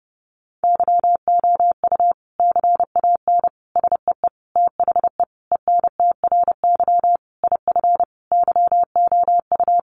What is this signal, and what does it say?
Morse code, heard as a keyed tone, about 30 wpm, 700 Hz. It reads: YOU CAN SEE THE ENTRY IF YOU
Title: Sentences from Top 1500 Words 30wpm